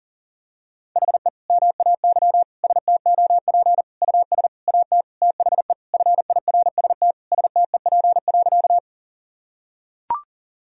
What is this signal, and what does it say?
Morse code, heard as a keyed tone, about 40 wpm, 700 Hz. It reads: HE MAY STOP US AT THE FIRST STEP.
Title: Sentences from Top 1000 Words - Review Part 2 - 40wpm